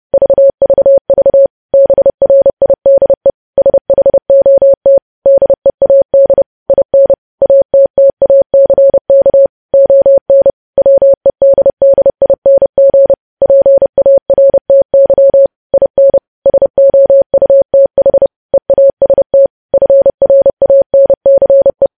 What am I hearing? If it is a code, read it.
Morse code, heard as a keyed tone, about 30 wpm, 550 Hz. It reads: VVV BRIDE SHOT DEAD IN ATTACK ON WEDDING PARTY IN SOUTH EAST FRANCE
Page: news headlines in morse code at 30 wpm